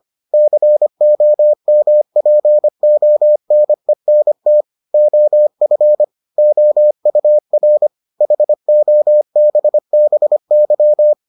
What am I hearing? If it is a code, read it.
Morse code, heard as a keyed tone, about 25 wpm, 600 Hz. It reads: COMPONENT OF OUR HOBBY